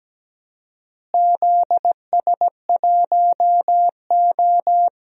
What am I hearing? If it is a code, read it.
Morse code, heard as a keyed tone, about 17 wpm, 700 Hz. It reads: ZS1O
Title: DXCC-100 Call Signs - 4 Characters - Repeated 3x - 17wpm